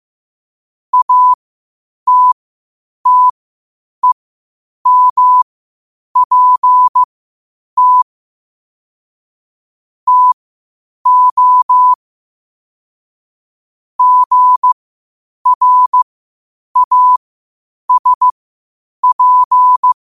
Morse code, heard as a keyed tone, about 15 wpm, spaced as for 8 wpm, 1000 Hz. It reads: ATTEMPT TO GRASP
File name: qotdfiles / 8wpm